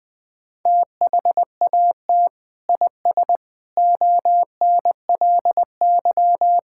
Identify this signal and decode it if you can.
Morse code, heard as a keyed tone, about 20 wpm, 700 Hz. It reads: THAT IS ONLY
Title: Sentences from Top 1000 Words - Review Part 5 - 20wpm